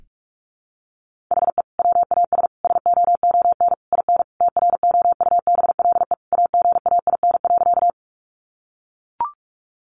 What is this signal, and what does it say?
Morse code, heard as a keyed tone, about 44 wpm, 700 Hz. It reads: HE WAS SOON IN TROUBLE AGAIN.
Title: Sentences from Top 1200 Words 45wpm